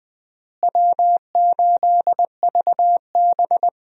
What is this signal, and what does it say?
Morse code, heard as a keyed tone, about 20 wpm, 700 Hz. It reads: W8VB
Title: USA Call Signs - 4 Characters - Repeated 3x - 20wpm